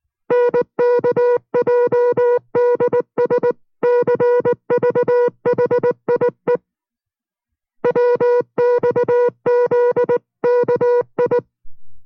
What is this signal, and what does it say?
Morse code, heard as a keyed tone, about 19 wpm, 490 Hz. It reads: NKJDS CVHIE WXZKI